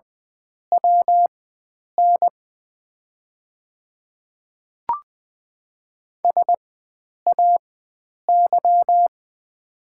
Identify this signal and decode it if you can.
Morse code, heard as a keyed tone, about 20 wpm, spaced as for 9 wpm, 700 Hz. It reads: WN SAY
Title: 4x Character Spacing - Top 300 Words - 20wpm